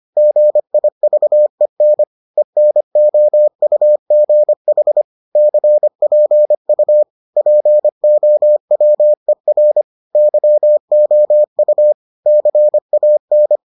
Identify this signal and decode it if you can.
Morse code, heard as a keyed tone, about 25 wpm, 600 Hz. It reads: GIVEN ENOUGH CPU POWER YOU CAN